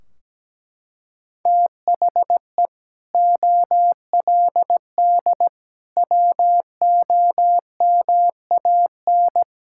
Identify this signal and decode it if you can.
Morse code, heard as a keyed tone, about 17 wpm, 700 Hz. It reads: THE OLD WOMAN
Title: Sentences from Top 1600 Words 17wpm